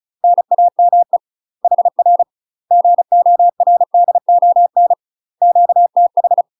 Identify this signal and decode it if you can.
Morse code, heard as a keyed tone, about 36 wpm, 700 Hz. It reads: NAME HR GORDON QTH
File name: UK MASTER QSOs TRAINING / 35WPM / UK MASTER QSOs 0036